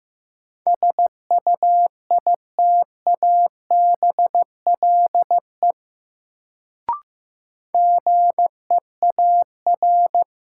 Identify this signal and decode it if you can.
Morse code, heard as a keyed tone, about 15 wpm, 700 Hz. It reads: SUITABLE GEAR
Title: Sentences from Top 1400 Words - New Word Drill Alt - 15wpm